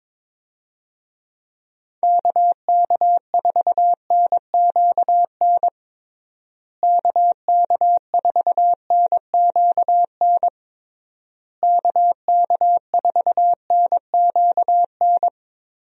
Morse code, heard as a keyed tone, about 22 wpm, 700 Hz. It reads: KK4NQN KK4NQN KK4NQN
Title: USA Call Signs - 6 Characters - Repeated 3x - 22wpm